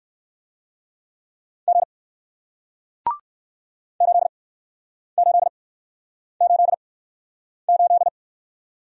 Morse code, heard as a keyed tone, about 59 wpm, spaced as for 9 wpm, 700 Hz. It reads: G 8888